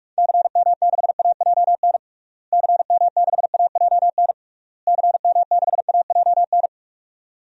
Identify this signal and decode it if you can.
Morse code, heard as a keyed tone, about 46 wpm, 700 Hz. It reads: CM6AJN CM6AJN CM6AJN
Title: DXCC-100 Call Signs - 6 Characters - Repeated 3x - 45wpm